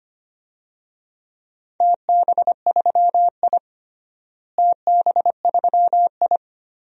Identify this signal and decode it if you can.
Morse code, heard as a keyed tone, about 25 wpm, 700 Hz. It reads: TB3I TB3I
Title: DXCC-100 Call Signs - 4 Characters - Repeated 3x - 25wpm